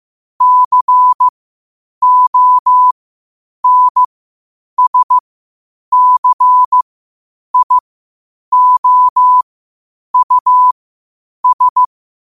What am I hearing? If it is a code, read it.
Morse code, heard as a keyed tone, about 15 wpm, spaced as for 8 wpm, 1000 Hz. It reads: CONSCIOUS
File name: qotdfiles / 8wpm